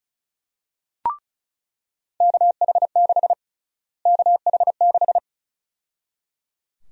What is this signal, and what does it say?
Morse code, heard as a keyed tone, about 35 wpm, 700 Hz. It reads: KH6 KH6
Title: USA Call Sign Prefixes - Repeated 2x - 35wpm